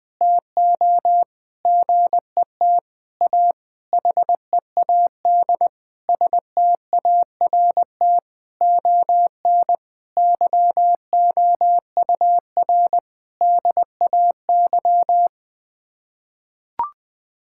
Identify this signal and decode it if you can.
Morse code, heard as a keyed tone, about 20 wpm, 700 Hz. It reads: TO GET A HEAD START ON YOUR DAY
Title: Sets of 9 Words - 20wpm